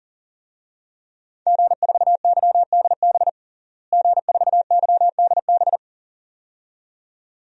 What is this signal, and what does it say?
Morse code, heard as a keyed tone, about 40 wpm, 700 Hz. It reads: G4YDB G4YDB
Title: DXCC-100 Call Signs - 5 Characters - Repeated 2x - 40wpm